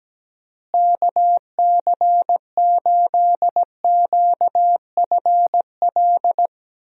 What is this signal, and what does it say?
Morse code, heard as a keyed tone, about 17 wpm, 700 Hz. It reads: KC8QFL
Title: USA Call Signs - 6 Characters - Repeated 3x - 17wpm